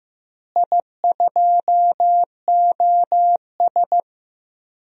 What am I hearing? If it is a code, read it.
Morse code, heard as a keyed tone, about 15 wpm, 700 Hz. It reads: I2OS
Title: DXCC-100 Call Signs - 4 Characters - Repeated 3x - 15wpm